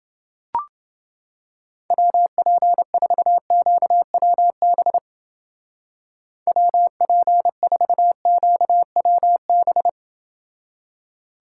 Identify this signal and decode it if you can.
Morse code, heard as a keyed tone, about 28 wpm, 700 Hz. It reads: WP4QWB WP4QWB